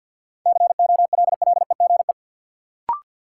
Morse code, heard as a keyed tone, about 53 wpm, 700 Hz. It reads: COLLEGE
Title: Sentences from Top 1300 Words - New Word Drill Alt - 50wpm